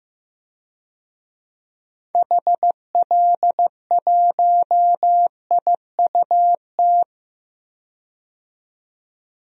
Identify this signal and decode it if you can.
Morse code, heard as a keyed tone, about 15 wpm, 700 Hz. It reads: HL1IUT
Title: DXCC-100 Call Signs - 6 Characters - 15wpm